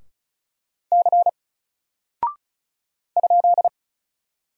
Morse code, heard as a keyed tone, about 35 wpm, 700 Hz. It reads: C ?
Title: Single Character (Letter:Number:Punctuation:Prosign) 35wpm